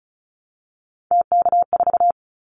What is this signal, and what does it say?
Morse code, heard as a keyed tone, about 36 wpm, 700 Hz. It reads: TK4
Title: Callsign Prefixes for Top 100 DXCC - 35wpm